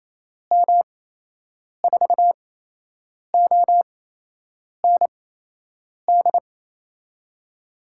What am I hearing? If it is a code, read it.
Morse code, heard as a keyed tone, about 28 wpm, spaced as for 8 wpm, 700 Hz. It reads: M4OND
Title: USA Call Signs - 6 Characters - 8x Character Spacing - 28wpm